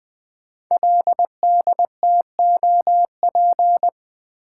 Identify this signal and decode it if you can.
Morse code, heard as a keyed tone, about 20 wpm, 700 Hz. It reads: LDTOP
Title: Code Groups - 5 Letters - 20wpm